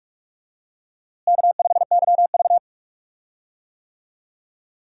Morse code, heard as a keyed tone, about 46 wpm, 700 Hz. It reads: K5YV
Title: USA Call Signs - 4 Characters - Repeated 2x - 45wpm